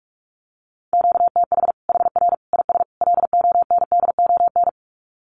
Morse code, heard as a keyed tone, about 44 wpm, 700 Hz. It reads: QTH HR IS LONDON
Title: Common QSO Phrases 45wpm